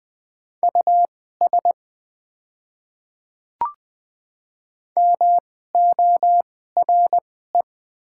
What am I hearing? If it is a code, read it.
Morse code, heard as a keyed tone, about 20 wpm, spaced as for 15 wpm, 700 Hz. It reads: US MORE
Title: 2x Character Spacing - Top 200 Words - 20wpm